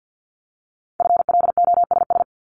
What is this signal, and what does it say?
Morse code, heard as a keyed tone, about 50 wpm, 700 Hz. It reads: FLOSS
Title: Multi-letter Phonemes 50wpm